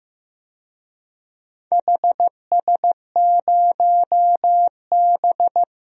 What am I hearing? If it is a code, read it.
Morse code, heard as a keyed tone, about 15 wpm, 700 Hz. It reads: HS0B
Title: DXCC-100 Call Signs - 4 Characters - 15wpm